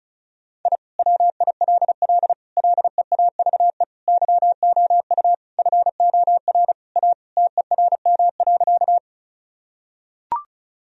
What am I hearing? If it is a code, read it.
Morse code, heard as a keyed tone, about 34 wpm, 700 Hz. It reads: I WILL LEAVE YOU FOR A TERM.
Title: Sentences from Top 1000 Words - Review Part 4 - 35wpm